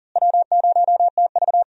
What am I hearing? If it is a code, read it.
Morse code, heard as a keyed tone, about 40 wpm, 700 Hz. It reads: W0TV
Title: USA Call Signs - 4 Characters - Repeated 3x - 40wpm